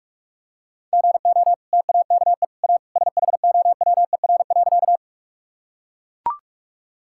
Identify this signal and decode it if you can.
Morse code, heard as a keyed tone, about 44 wpm, 700 Hz. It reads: GO TAKE A SHOWER.
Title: Sentences from Top 1600 Words 45wpm